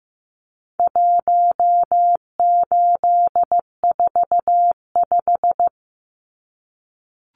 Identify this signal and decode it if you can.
Morse code, heard as a keyed tone, about 15 wpm, 700 Hz. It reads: 1845
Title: Numbers - 4-Digit 15wpm